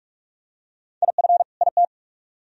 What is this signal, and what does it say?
Morse code, heard as a keyed tone, about 43 wpm, 700 Hz. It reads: IF IT